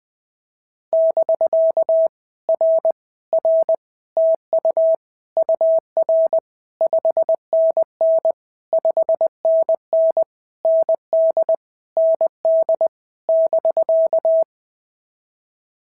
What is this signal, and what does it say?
Morse code, heard as a keyed tone, about 20 wpm, 650 Hz. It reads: <BK> R R TU UR 5NN 5NN ND ND <BK>
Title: POTA QSO Practice - 20wpm